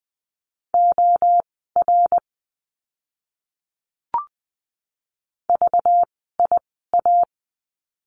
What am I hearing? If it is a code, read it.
Morse code, heard as a keyed tone, about 20 wpm, spaced as for 14 wpm, 700 Hz. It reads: OR VIA
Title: 192 Lesson 31f - Letter v - Words 20wpm